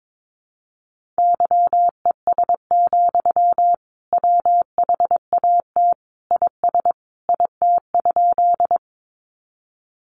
Morse code, heard as a keyed tone, about 22 wpm, 700 Hz. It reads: YES, WHAT IS IT?
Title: Sentences from Top 500 Words 22wpm